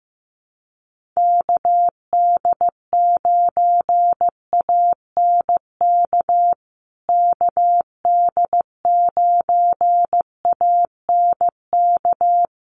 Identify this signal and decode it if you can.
Morse code, heard as a keyed tone, about 15 wpm, 700 Hz. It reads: KD9ANK KD9ANK
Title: USA Call Signs Repeated 3x 15wpm